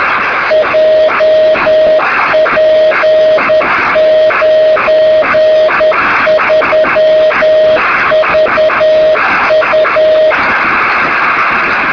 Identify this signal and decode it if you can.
Morse code, heard as a keyed tone, about 10 wpm, 620 Hz, with noise. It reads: JP93VU
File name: sm7lcb / 2203z sk2shf-b 23cm